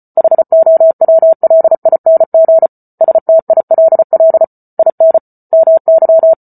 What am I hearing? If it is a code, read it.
Morse code, heard as a keyed tone, about 34 wpm, 650 Hz. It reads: HOWLING STILL IN MY